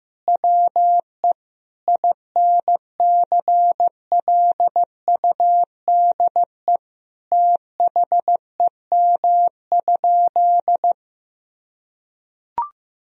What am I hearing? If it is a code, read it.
Morse code, heard as a keyed tone, about 15 wpm, 700 Hz. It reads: WE INCLUDE THEM?
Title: Sentences from Top 1000 Words - Review Part 3 - 15wpm